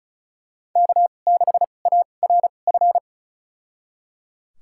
Morse code, heard as a keyed tone, about 34 wpm, spaced as for 25 wpm, 700 Hz. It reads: K6ARF
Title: USA Call Signs - 5 Characters - 2x Character Spacing - 35wpm